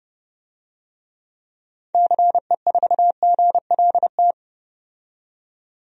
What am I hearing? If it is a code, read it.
Morse code, heard as a keyed tone, about 30 wpm, 700 Hz. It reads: CE4GLT
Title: DXCC-100 Call Signs - 6 Characters - 30wpm